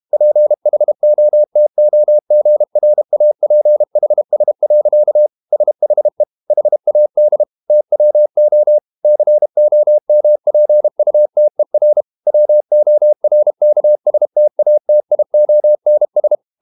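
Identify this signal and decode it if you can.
Morse code, heard as a keyed tone, about 32 wpm, 600 Hz. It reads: PHOTOGRAPHS. SHE HAD TWO COMPUTER WORKSTATIONS
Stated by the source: Morse code practice at 32 wpm from CW med Gnister, reading Ken Follett's novel Never in English.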